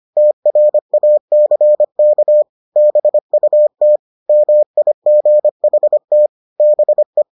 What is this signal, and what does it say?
Morse code, heard as a keyed tone, about 25 wpm, 600 Hz. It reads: TRACK BUT MIGHT BE